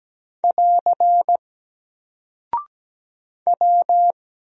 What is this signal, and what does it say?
Morse code, heard as a keyed tone, about 17 wpm, 700 Hz. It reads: <AR> W
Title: Single Character (Letter:Number:Punctuation:Prosign) Alt 17wpm